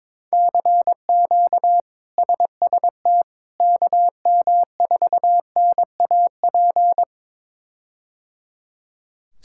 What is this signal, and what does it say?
Morse code, heard as a keyed tone, about 22 wpm, 700 Hz. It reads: CQ SST KM4NAP
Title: K1USN SST QSO Practice - 22wpm